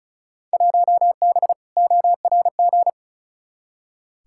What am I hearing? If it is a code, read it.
Morse code, heard as a keyed tone, about 36 wpm, 700 Hz. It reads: 1B ORG
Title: Field Day Exchanges Encore - 35wpm